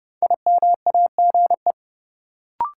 Morse code, heard as a keyed tone, about 30 wpm, 700 Hz. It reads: IMAGE
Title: Sentences from Top 700 Words - New Word Drill - 30wpm